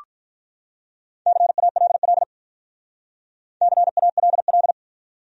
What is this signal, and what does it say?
Morse code, heard as a keyed tone, about 44 wpm, 700 Hz. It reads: CALL CALL